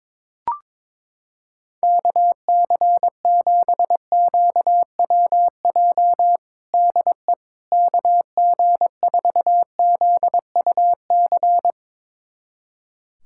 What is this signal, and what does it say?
Morse code, heard as a keyed tone, about 22 wpm, 700 Hz. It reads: KC7QWJ DE KG4ZUC
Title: USA Callsign DE Callsign - 6 Characters - 22wpm